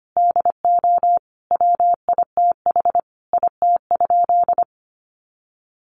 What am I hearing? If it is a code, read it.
Morse code, heard as a keyed tone, about 25 wpm, 700 Hz. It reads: DO WITH IT?
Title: Sentences from Top 100 Words 25wpm